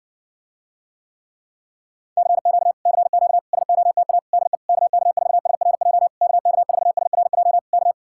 Morse code, heard as a keyed tone, about 60 wpm, 700 Hz. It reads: CQ CQ SOTA DE KK4SRJ KK4SRJ K